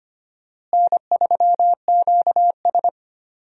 Morse code, heard as a keyed tone, about 25 wpm, 700 Hz. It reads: N3QS
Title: USA Call Signs - 4 Characters - Repeated 3x - 25wpm